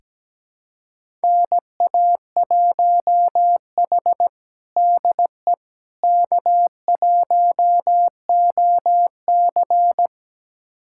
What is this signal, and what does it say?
Morse code, heard as a keyed tone, about 17 wpm, 700 Hz. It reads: NA1H DE K1OC
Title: USA Callsign DE Callsign - 4 Characters - 17wpm